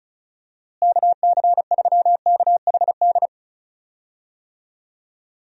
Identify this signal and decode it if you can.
Morse code, heard as a keyed tone, about 36 wpm, 700 Hz. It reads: KC3KHD
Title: USA Call Signs - 6 Characters - Repeated 2x - 35wpm